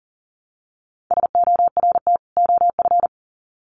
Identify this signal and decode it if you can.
Morse code, heard as a keyed tone, about 40 wpm, 700 Hz. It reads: SORT OF
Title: Sets of 2 Words 40wpm